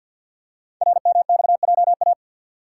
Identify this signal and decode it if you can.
Morse code, heard as a keyed tone, about 50 wpm, 700 Hz. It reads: RMXJA
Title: Code Groups - 5 Letters - 50wpm